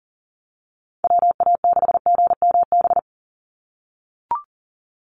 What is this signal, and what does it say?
Morse code, heard as a keyed tone, about 39 wpm, 700 Hz. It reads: WA6GMB
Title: USA Call Signs 40wpm